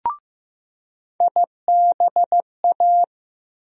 Morse code, heard as a keyed tone, about 15 wpm, 700 Hz. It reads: IBA
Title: USA Call Sign Postfixes - 3 Letters 15wpm